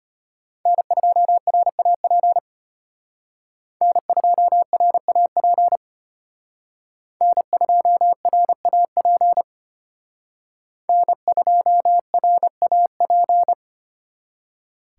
Morse code, heard as a keyed tone, about 32 wpm, 700 Hz. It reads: N2RAP N2RAP N2RAP N2RAP